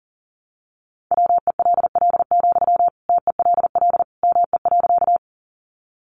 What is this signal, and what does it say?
Morse code, heard as a keyed tone, about 40 wpm, 700 Hz. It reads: WELL, TELL ME.
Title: Sentences from Top 200 Words 40wpm